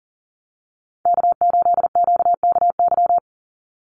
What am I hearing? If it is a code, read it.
Morse code, heard as a keyed tone, about 40 wpm, 700 Hz. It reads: K8QKY
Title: Call Signs Repeated 3x - 40wpm